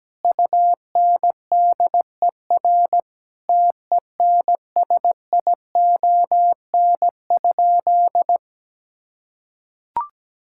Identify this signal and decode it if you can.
Morse code, heard as a keyed tone, about 17 wpm, 700 Hz. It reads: UNDER TENSION?
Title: Sentences from Top 1600 Words 17wpm